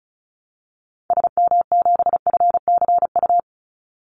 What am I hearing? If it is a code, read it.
Morse code, heard as a keyed tone, about 34 wpm, 700 Hz. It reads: SM7FCU
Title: Call Signs Repeated 2x - 35wpm